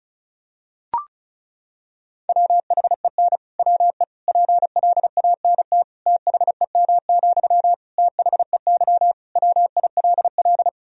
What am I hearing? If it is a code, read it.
Morse code, heard as a keyed tone, about 36 wpm, 700 Hz. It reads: WHEN WE PLANT THEM, THEY WILL
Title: Sentences from Top 1000 Words - Review Part 1 - 35wpm